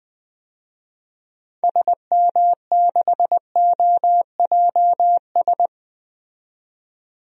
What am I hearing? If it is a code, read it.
Morse code, heard as a keyed tone, about 20 wpm, 700 Hz. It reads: SM6OJS
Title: DXCC-100 Call Signs - 6 Characters - 20wpm